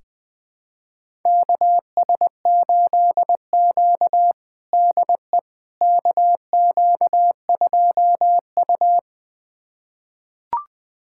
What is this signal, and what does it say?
Morse code, heard as a keyed tone, about 20 wpm, 700 Hz. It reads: KS8Q DE KQ2U
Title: USA Callsign DE Callsign - 4 Characters - 20wpm